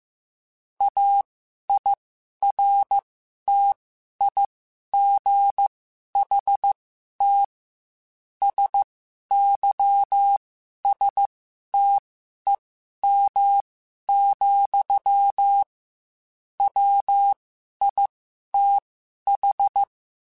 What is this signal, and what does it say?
Morse code, heard as a keyed tone, about 15 wpm, spaced as for 11 wpm, 800 Hz. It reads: AIRTIGHT SYSTEM, WITH